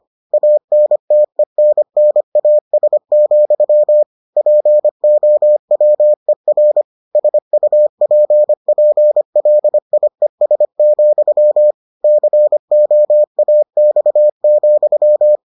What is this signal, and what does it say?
Morse code, heard as a keyed tone, about 25 wpm, 600 Hz. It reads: ANTENNAS, POWER SUPPLIES, COAX,